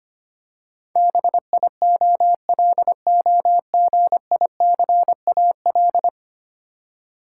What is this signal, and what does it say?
Morse code, heard as a keyed tone, about 25 wpm, 700 Hz. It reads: BIOLOGICAL
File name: 10 Letter Words - First 3 Letters Given - 25wpm